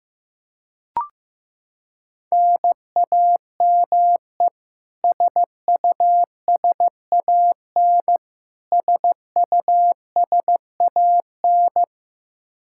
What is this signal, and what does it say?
Morse code, heard as a keyed tone, about 15 wpm, 700 Hz. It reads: NAME SUSAN SUSAN
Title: Practice QSO - Exchange 1 - Beginning - 15wpm